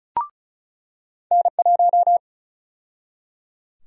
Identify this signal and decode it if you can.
Morse code, heard as a keyed tone, about 34 wpm, 700 Hz. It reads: N1
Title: USA Call Sign Prefixes - 35wpm